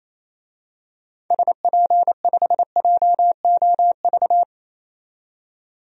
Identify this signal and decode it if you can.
Morse code, heard as a keyed tone, about 28 wpm, 700 Hz. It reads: SP5JOV
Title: DXCC-100 Call Signs - 6 Characters - 28wpm